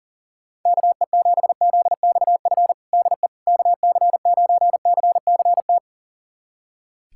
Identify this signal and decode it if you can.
Morse code, heard as a keyed tone, about 40 wpm, 700 Hz. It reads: KE7ZXF DE KC9CCT
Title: USA Callsign DE Callsign - 6 Characters - 40wpm